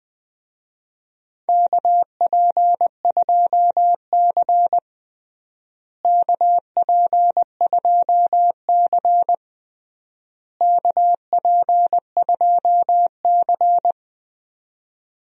Morse code, heard as a keyed tone, about 21 wpm, 700 Hz. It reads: KP2C KP2C KP2C
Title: DXCC-100 Call Signs - 4 Characters - Repeated 3x - 20wpm